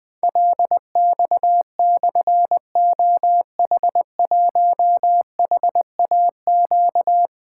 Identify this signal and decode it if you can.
Morse code, heard as a keyed tone, about 20 wpm, 700 Hz. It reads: LX/OH1HAQ
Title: Difficult Call Signs 20wpm